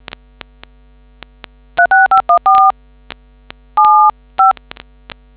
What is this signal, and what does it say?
DTMF, heard as telephone touch tones, tones of about 140 ms. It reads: 36514 75